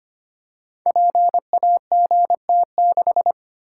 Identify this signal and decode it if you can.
Morse code, heard as a keyed tone, about 25 wpm, 690 Hz. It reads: PAGT6
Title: Code Groups - 5 Letter:Numbers - 25wpm